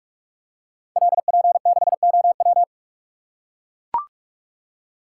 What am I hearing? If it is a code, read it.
Morse code, heard as a keyed tone, about 44 wpm, 700 Hz. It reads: LP6OW
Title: Code Groups - 5 Letter:Numbers - 45wpm